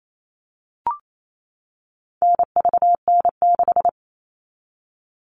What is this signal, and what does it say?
Morse code, heard as a keyed tone, about 28 wpm, 700 Hz. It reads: NVN6
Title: All Cut Numbers - 4-Digit 28wpm